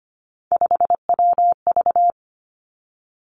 Morse code, heard as a keyed tone, about 25 wpm, 700 Hz. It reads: 5WV
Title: Three Challenging Letter-Number Combinations 25wpm